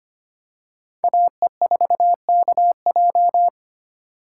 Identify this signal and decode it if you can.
Morse code, heard as a keyed tone, about 25 wpm, 700 Hz. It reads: AE4KJ
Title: USA Call Signs - 5 Characters - 2x Character Spacing - 25wpm